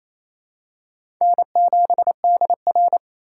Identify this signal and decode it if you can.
Morse code, heard as a keyed tone, about 28 wpm, 700 Hz. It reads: N7DR
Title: K1USN SST QSO Practice - 28wpm